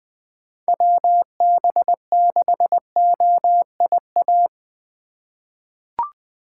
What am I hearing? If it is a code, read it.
Morse code, heard as a keyed tone, about 20 wpm, 700 Hz. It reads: WB6OIA
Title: USA Call Signs - 6 Characters - 2x Character Spacing - 20wpm